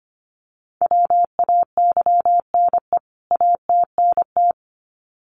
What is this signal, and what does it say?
Morse code, heard as a keyed tone, about 25 wpm, 700 Hz. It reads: WAYNE ATNT
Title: CWT Exchange 25wpm